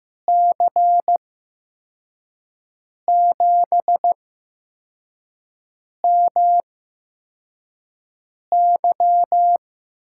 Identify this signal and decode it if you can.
Morse code, heard as a keyed tone, about 15 wpm, spaced as for 4 wpm, 700 Hz. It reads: C7MY
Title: USA Call Signs - 5 Characters - 8x Character Spacing - 15wpm